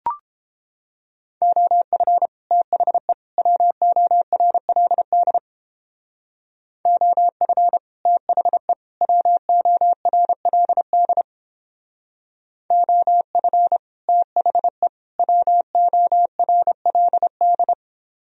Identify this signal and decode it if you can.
Morse code, heard as a keyed tone, about 29 wpm, 700 Hz. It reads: OF THE WORLD OF THE WORLD OF THE WORLD